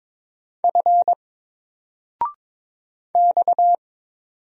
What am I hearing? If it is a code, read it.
Morse code, heard as a keyed tone, about 22 wpm, 700 Hz. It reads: F X
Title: Single Letter-Number 22wpm